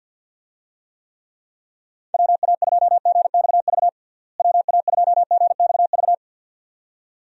Morse code, heard as a keyed tone, about 52 wpm, 700 Hz. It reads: WA2GXV WA2GXV
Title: USA Call Signs - 6 Characters - Repeated 2x - 50wpm